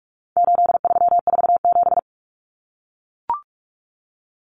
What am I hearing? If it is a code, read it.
Morse code, heard as a keyed tone, about 44 wpm, 700 Hz. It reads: 8347
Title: Numbers - 4-Digit 45wpm